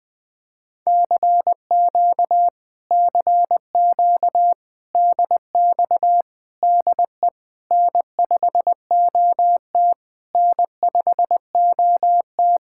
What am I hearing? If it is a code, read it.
Morse code, heard as a keyed tone, about 20 wpm, 700 Hz. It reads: CQ CQ DX DE N5OT N5OT
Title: Speed-Racing Common QSO Phrases Encore - 17wpm